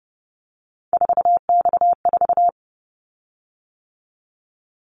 Morse code, heard as a keyed tone, about 32 wpm, 700 Hz. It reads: 4X4
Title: Callsign Prefixes for Top 50 DXCC - 30wpm